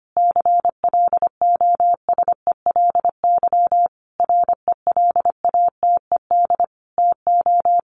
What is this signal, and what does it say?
Morse code, heard as a keyed tone, about 25 wpm, 700 Hz. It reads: CLOSELY RELATED TO
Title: Sentences from Top 1400 Words 25wpm